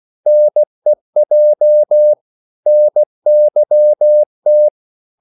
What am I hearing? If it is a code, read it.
Morse code, heard as a keyed tone, about 16 wpm, 600 Hz. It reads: NEJ NYT
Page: Korte danske ord 16wpm